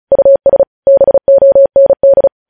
Morse code, heard as a keyed tone, about 36 wpm, 550 Hz. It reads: US BOND
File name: podcast / News headlines-1x-35x35